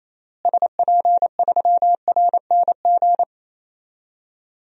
Morse code, heard as a keyed tone, about 28 wpm, 700 Hz. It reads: SP3RNG
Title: DXCC-100 Call Signs - 6 Characters - Repeated 3x - 28wpm